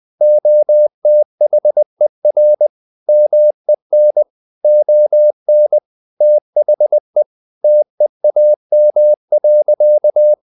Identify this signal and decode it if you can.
Morse code, heard as a keyed tone, about 20 wpm, 600 Hz. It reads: OTHER MEN ON THE TEAM.
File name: the old man and the sea / the old man and the sea part 1 20wpm